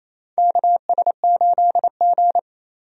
Morse code, heard as a keyed tone, about 27 wpm, 700 Hz. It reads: KS8G
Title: USA Call Signs - 4 Characters - 28wpm